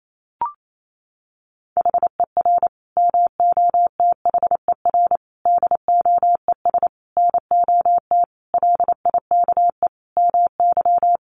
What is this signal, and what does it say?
Morse code, heard as a keyed tone, about 28 wpm, 700 Hz. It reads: HER MOTHER DOES NOT LIKE MY
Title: Sentences from Top 1400 Words 28wpm